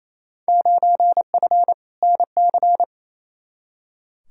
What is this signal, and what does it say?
Morse code, heard as a keyed tone, about 28 wpm, 700 Hz. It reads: 9F NC
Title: Field Day Exchanges Encore - 28wpm